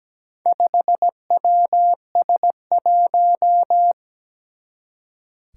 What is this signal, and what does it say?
Morse code, heard as a keyed tone, about 17 wpm, 700 Hz. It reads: 5WS1
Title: Four Challenging Letter-Number Combinations 17wpm